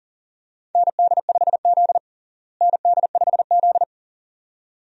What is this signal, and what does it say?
Morse code, heard as a keyed tone, about 40 wpm, 700 Hz. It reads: ND5Z ND5Z
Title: USA Call Signs - 4 Characters - Repeated 2x - 40wpm